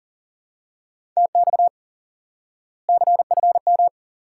Morse code, heard as a keyed tone, about 40 wpm, 700 Hz. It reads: TX CFM
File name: IQE Course 3 - Lesson 89 - YAGI - Two Elements - 40wpm